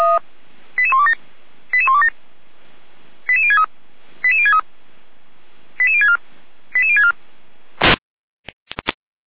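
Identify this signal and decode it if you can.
Five-tone selective call (ZVEI), heard as quick runs of five tones, tones of about 70 ms. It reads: 88127 88127 88063 88063 88064 88064
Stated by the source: SDR-received ZVEI five-tone selective call 88129, recorded by feuerwehr-orlamuende (fire brigade Orlamünde).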